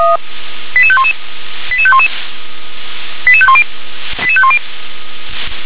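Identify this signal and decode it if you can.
Five-tone selective call (ZVEI), heard as quick runs of five tones, tones of about 70 ms. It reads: 88511 88511 88510 88510
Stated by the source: ZVEI five-tone call received on an SDR.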